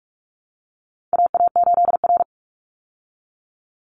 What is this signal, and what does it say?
Morse code, heard as a keyed tone, about 44 wpm, 700 Hz. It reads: AA8R
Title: Call Signs Repeated 3x - 45wpm